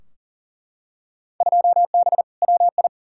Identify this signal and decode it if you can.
Morse code, heard as a keyed tone, about 40 wpm, 700 Hz. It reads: 2B WI
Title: Field Day Exchanges - 40wpm